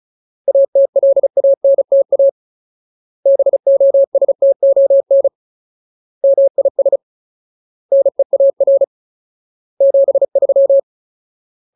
Morse code, heard as a keyed tone, about 34 wpm, 550 Hz. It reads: ATLANTA BOSTON MIS NEAR 73
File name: Ultimate Word List / 35WPM Ultimate